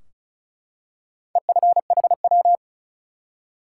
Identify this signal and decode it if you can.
Morse code, heard as a keyed tone, about 36 wpm, 700 Hz. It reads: EFHW
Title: Top 180 QSO Elements 35wpm